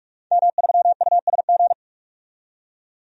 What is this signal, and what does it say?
Morse code, heard as a keyed tone, about 44 wpm, 700 Hz. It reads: M3USG
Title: DXCC-100 Call Signs - 5 Characters - 45wpm